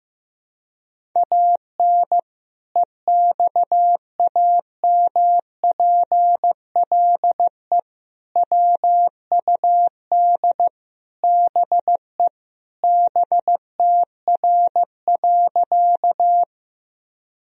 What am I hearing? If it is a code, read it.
Morse code, heard as a keyed tone, about 15 wpm, 700 Hz. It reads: AN EXAMPLE WUD BE BTR.
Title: Sentences from Top 400 Words and Abbreviations 15wpm